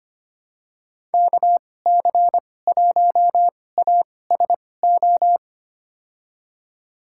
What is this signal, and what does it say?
Morse code, heard as a keyed tone, about 25 wpm, spaced as for 18 wpm, 700 Hz. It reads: KC1ASO
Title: USA Call Signs - 2x Character Spacing - 25wpm